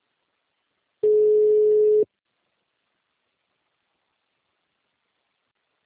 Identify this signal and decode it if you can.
Ringback tone (Europe, 425 Hz).
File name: out- -2019- - - .80599